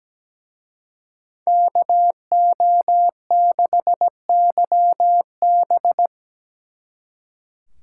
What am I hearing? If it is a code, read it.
Morse code, heard as a keyed tone, about 20 wpm, 700 Hz. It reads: KO6YB
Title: Speed Racing - USA 2x2 Call Signs 17wpm